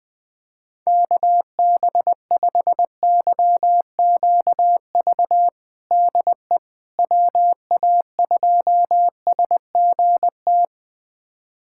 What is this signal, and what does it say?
Morse code, heard as a keyed tone, about 20 wpm, 700 Hz. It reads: KB5YQV DE WA2SGT
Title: USA Callsign DE Callsign 20wpm